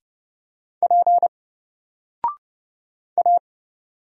Morse code, heard as a keyed tone, about 30 wpm, 700 Hz. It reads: P A
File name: Single-Letter-Number-30wpm